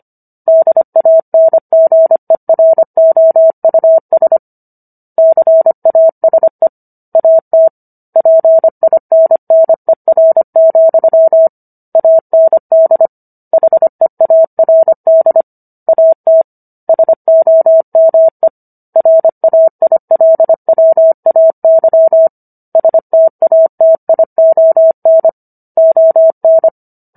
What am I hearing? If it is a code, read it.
Morse code, heard as a keyed tone, about 25 wpm, 650 Hz. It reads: DANGEROUS CASE AT PINNER, AND HEARD AT SOME RAILWAY STATION ON